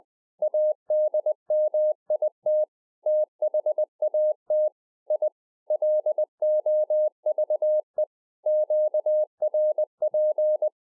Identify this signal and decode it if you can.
Morse code, heard as a keyed tone, about 20 wpm, 610 Hz. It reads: ADMIT THAT I LOVE QRP
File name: 2011-04-mp3 / A SMALL TRANCEIVER 3 20WPM